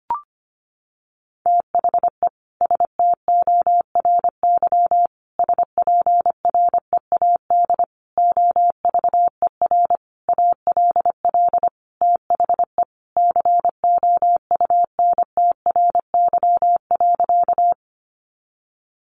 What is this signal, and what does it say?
Morse code, heard as a keyed tone, about 25 wpm, 700 Hz. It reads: THE STORY SPREAD OVER ALL THE COUNTRY.
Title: Sentences from Top 1200 Words 25wpm